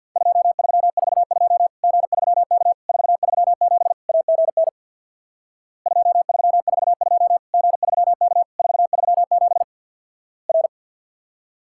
Morse code, heard as a keyed tone, about 50 wpm, 700 Hz. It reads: 2342 G3K 437 AGN 2342 G3K 437 R
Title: Worked All Europe DX Contest - QTC Exchange with Repeats - 50wpm